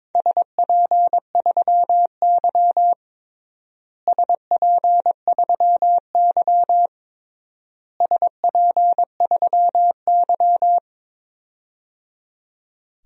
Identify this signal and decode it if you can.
Morse code, heard as a keyed tone, about 21 wpm, 700 Hz. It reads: SP3Y SP3Y SP3Y
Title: DXCC-100 Call Signs - 4 Characters - Repeated 3x - 22wpm